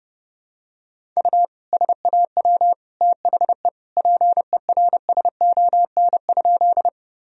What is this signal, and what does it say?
Morse code, heard as a keyed tone, about 30 wpm, 700 Hz. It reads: U SAW THE PERSON?
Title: Sentences from Top 400 Words and Abbreviations 30wpm